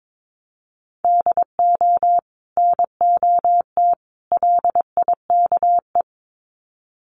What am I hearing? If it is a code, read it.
Morse code, heard as a keyed tone, about 22 wpm, 700 Hz. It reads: DO NOT LIKE
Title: Sets of 3 Words 22wpm